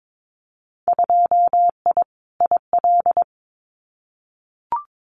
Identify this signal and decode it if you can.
Morse code, heard as a keyed tone, about 22 wpm, 700 Hz. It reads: 2I IL
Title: Winter Field Day Exchanges Encore 22wpm